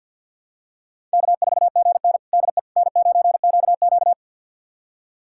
Morse code, heard as a keyed tone, about 51 wpm, 700 Hz. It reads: K4GN DE N9QQ
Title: USA Callsign DE Callsign - 4 Characters - 50wpm